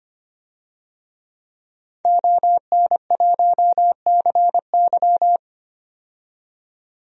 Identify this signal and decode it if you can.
Morse code, heard as a keyed tone, about 25 wpm, 700 Hz. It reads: ON1CY
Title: DXCC-100 Call Signs - 5 Characters - 25wpm